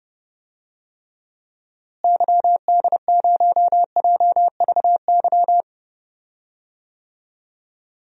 Morse code, heard as a keyed tone, about 32 wpm, 700 Hz. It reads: YD0JVY
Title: DXCC-100 Call Signs - 6 Characters - Repeated 2x - 30wpm